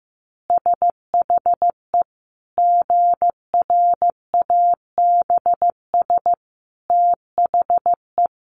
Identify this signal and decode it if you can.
Morse code, heard as a keyed tone, about 15 wpm, 700 Hz. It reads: SHE GRABS THE
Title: Easy Phrases 15wpm